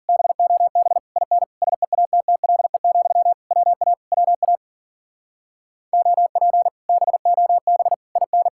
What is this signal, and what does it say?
Morse code, heard as a keyed tone, about 49 wpm, 700 Hz. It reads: BOB IN SEATTLE, WA WA OP BOB IN
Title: Speed-Racing Common QSO Phrases - 40wpm